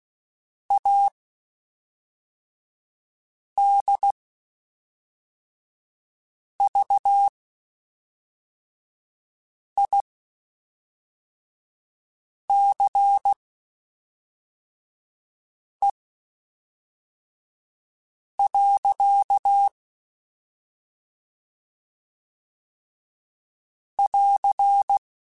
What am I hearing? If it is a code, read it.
Morse code, heard as a keyed tone, about 16 wpm, spaced as for 3 wpm, 800 Hz. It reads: ADVICE. <AR>
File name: Quotes Spaced / Quotes Spaced-mp3s / Advice sp @5